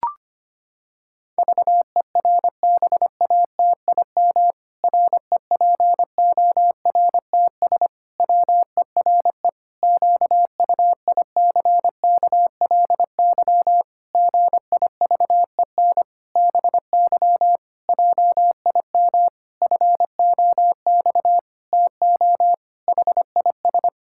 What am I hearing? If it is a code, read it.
Morse code, heard as a keyed tone, about 25 wpm, 700 Hz. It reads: VERBATIM REPORTS WERE QUICKLY GIVEN BY JIM FOX TO HIS
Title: Pangram Challenge v2 - 25wpm